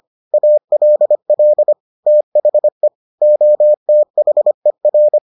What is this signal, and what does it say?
Morse code, heard as a keyed tone, about 25 wpm, 600 Hz. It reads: ALL THE OTHER